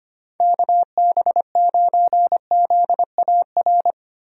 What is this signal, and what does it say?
Morse code, heard as a keyed tone, about 25 wpm, 700 Hz. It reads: KB9ZAR
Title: USA Call Signs - 6 Characters - Repeated 3x - 25wpm